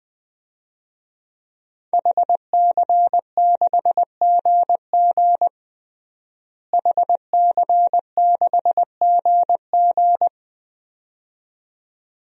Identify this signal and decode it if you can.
Morse code, heard as a keyed tone, about 21 wpm, 700 Hz. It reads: HC6GG HC6GG
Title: DXCC-100 Call Signs - 5 Characters - Repeated 2x - 20wpm